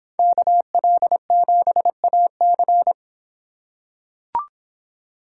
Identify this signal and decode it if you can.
Morse code, heard as a keyed tone, about 25 wpm, 700 Hz. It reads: KL7AC